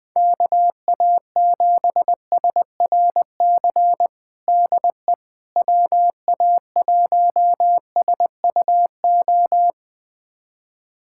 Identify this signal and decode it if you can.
Morse code, heard as a keyed tone, about 21 wpm, 700 Hz. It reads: KA7SRC DE WA1SUO
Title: USA Callsign DE Callsign - 6 Characters - 20wpm